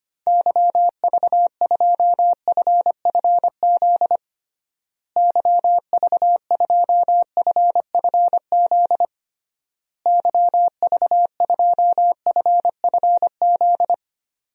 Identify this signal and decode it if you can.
Morse code, heard as a keyed tone, about 25 wpm, 700 Hz. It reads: YV2FFZ YV2FFZ YV2FFZ
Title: DXCC-100 Call Signs - 6 Characters - Repeated 3x - 25wpm